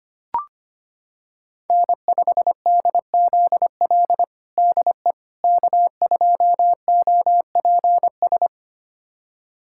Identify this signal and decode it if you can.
Morse code, heard as a keyed tone, about 25 wpm, 700 Hz. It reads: N5DZL DE K2OPS
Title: USA Callsign DE Callsign - 5 Characters - 25wpm